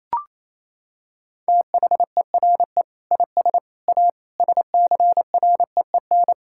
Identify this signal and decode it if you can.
Morse code, heard as a keyed tone, about 27 wpm, 700 Hz. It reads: THERE IS A SCREEN
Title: Sentences from Top 1000 Words - Review Part 6 - 28wpm